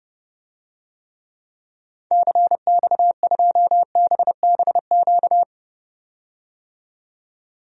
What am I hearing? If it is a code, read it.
Morse code, heard as a keyed tone, about 32 wpm, 700 Hz. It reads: CX2BBQ
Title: DXCC-100 Call Signs - 6 Characters - 30wpm